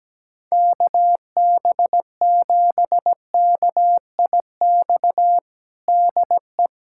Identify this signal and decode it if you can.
Morse code, heard as a keyed tone, about 17 wpm, 700 Hz. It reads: KB7KIX DE
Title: USA Callsign DE Callsign 17wpm